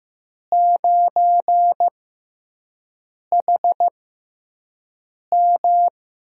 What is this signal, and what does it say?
Morse code, heard as a keyed tone, about 15 wpm, spaced as for 5 wpm, 700 Hz. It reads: 9HM
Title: USA Call Signs - 5 Characters - 6x Character Spacing - 15wpm